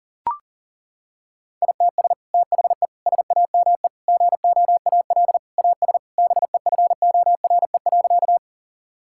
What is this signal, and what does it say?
Morse code, heard as a keyed tone, about 40 wpm, 700 Hz. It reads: ITS THE SAME GOAL AS BEFORE.
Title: Sentences from Top 1000 Words - Review Part 5 - 40wpm